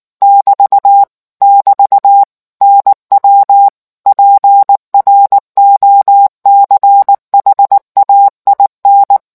Morse code, heard as a keyed tone, about 19 wpm, 800 Hz. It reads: <BT> <BT> NW PROCHAIN